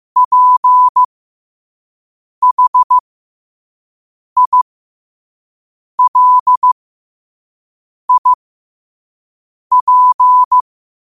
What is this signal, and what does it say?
Morse code, heard as a keyed tone, about 15 wpm, spaced as for 5 wpm, 1000 Hz. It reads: PHILIP
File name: qotdfiles / 5wpm